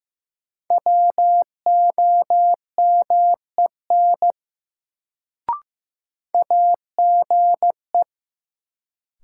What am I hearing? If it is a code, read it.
Morse code, heard as a keyed tone, about 16 wpm, 700 Hz. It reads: WOMEN AGE
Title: Sets of 3 Words - New Word Drill - 15wpm